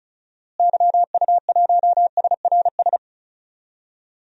Morse code, heard as a keyed tone, about 34 wpm, 700 Hz. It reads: YU1SRS
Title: DXCC-100 Call Signs - 6 Characters - Repeated 2x - 35wpm